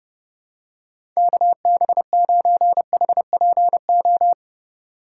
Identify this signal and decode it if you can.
Morse code, heard as a keyed tone, about 30 wpm, 700 Hz. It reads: KB9HPO
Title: USA Call Signs - 6 Characters - 6x Character Spacing - 30wpm